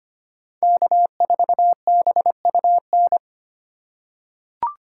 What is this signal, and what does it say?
Morse code, heard as a keyed tone, about 25 wpm, 700 Hz. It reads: K4BUN
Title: USA Call Signs - 8x Character Spacing - 25wpm